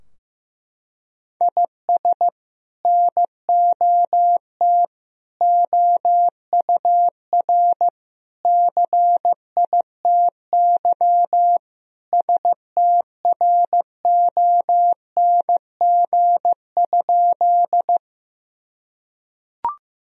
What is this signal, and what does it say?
Morse code, heard as a keyed tone, about 15 wpm, 700 Hz. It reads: IS NOT OUR CITY STRONG?
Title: Sentences from Top 1000 Words - Review Part 3 - 15wpm